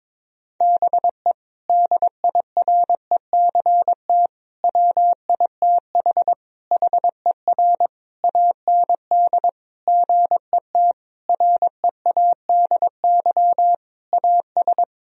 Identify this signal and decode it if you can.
Morse code, heard as a keyed tone, about 22 wpm, 700 Hz. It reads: BE DIRECT WITH HER AND GET READY AS
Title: Sentences from Top 1000 Words - Review Part 2 - 22wpm